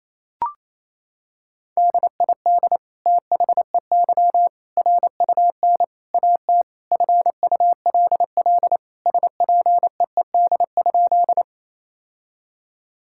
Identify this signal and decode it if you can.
Morse code, heard as a keyed tone, about 28 wpm, 700 Hz. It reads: DID THEY RUN AT FULL SPEED?
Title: Sentences from Top 1000 Words - Review Part 5 - 28wpm